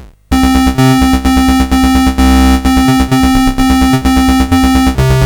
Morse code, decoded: none